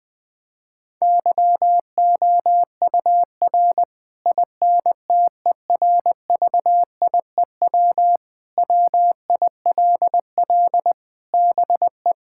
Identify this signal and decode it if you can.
Morse code, heard as a keyed tone, about 20 wpm, 700 Hz. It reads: YOUR INTERVIEW WILL BE
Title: Sentences from Top 1600 Words 20wpm